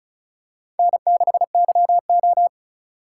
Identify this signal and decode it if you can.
Morse code, heard as a keyed tone, about 34 wpm, 700 Hz. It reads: N6YO
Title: USA Call Signs - 4 Characters - 4x Character Spacing - 35wpm